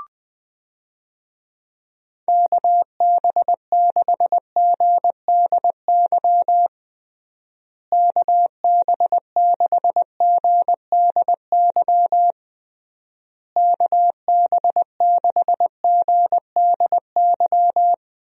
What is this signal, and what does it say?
Morse code, heard as a keyed tone, about 20 wpm, 700 Hz. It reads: KB6GDY KB6GDY KB6GDY
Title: USA Call Signs - 6 Characters - Repeated 3x - 20wpm